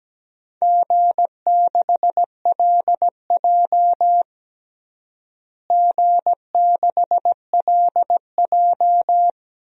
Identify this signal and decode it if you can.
Morse code, heard as a keyed tone, about 17 wpm, 700 Hz. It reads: G6LJ G6LJ
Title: DXCC-100 Call Signs - 4 Characters - Repeated 3x - 17wpm